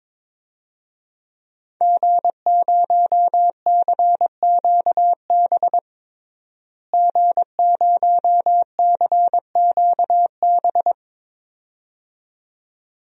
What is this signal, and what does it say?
Morse code, heard as a keyed tone, about 22 wpm, 700 Hz. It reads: G0CQB G0CQB
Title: DXCC-100 Call Signs - 5 Characters - Repeated 2x - 22wpm